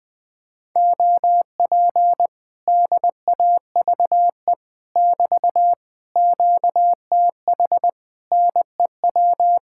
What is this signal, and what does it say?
Morse code, heard as a keyed tone, about 21 wpm, 700 Hz. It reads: OP DAVE <BT> QTH NEW
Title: Speed-Racing Common QSO Phrases Encore - 20wpm